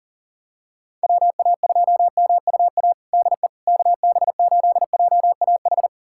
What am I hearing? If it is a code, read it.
Morse code, heard as a keyed tone, about 40 wpm, 700 Hz. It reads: WA2MUA DE KB8JAH
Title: USA Callsign DE Callsign 40wpm